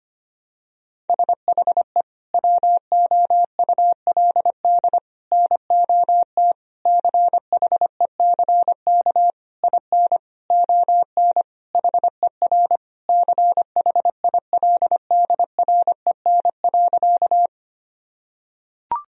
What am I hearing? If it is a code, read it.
Morse code, heard as a keyed tone, about 25 wpm, 700 Hz. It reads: SHE WOULD NOT CHECK IN ON HER CHILDREN.
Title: Sentences from Top 1000 Words - Review Part 3 - 25wpm